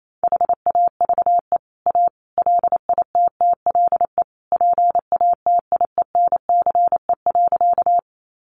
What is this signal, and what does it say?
Morse code, heard as a keyed tone, about 28 wpm, 700 Hz. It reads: HAVE A LITTLE PATIENCE.
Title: Sentences from Top 1300 Words 28wpm